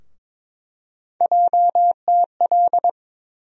Morse code, heard as a keyed tone, about 23 wpm, 700 Hz. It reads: JTL
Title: USA Call Sign Postfixes - 3 Letters 22wpm